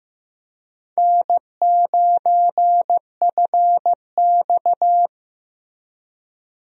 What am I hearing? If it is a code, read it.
Morse code, heard as a keyed tone, about 15 wpm, 700 Hz. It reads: N9FX
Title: USA Call Signs - 4 Characters - Repeated 2x - 15wpm